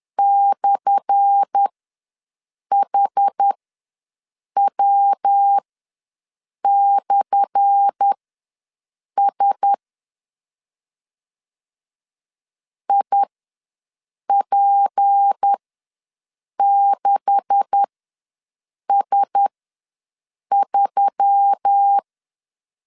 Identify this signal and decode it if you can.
Morse code, heard as a keyed tone, about 10 wpm, spaced as for 6 wpm, 800 Hz. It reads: /HW/S IP6S3
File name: LEZIONI / 10 SIGNS / 10A SLASH ws 10wpm 4121